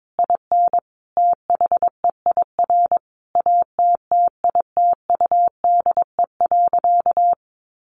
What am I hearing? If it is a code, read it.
Morse code, heard as a keyed tone, about 22 wpm, 700 Hz. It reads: IN THEIR ATTITUDE.
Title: Sentences from Top 1300 Words 22wpm